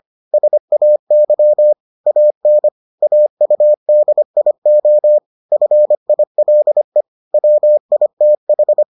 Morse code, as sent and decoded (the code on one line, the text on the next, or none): ... .- -.-- / .- -. / .- ..- -.. .. --- / ..-. .. .-.. / .-- .. - ....
SAY AN AUDIO FILE WITH